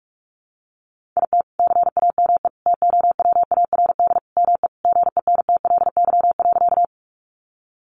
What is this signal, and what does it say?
Morse code, heard as a keyed tone, about 44 wpm, 700 Hz. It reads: IT CAME TOWARD ME GENTLY.
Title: Sentences from Top 1500 Words 45wpm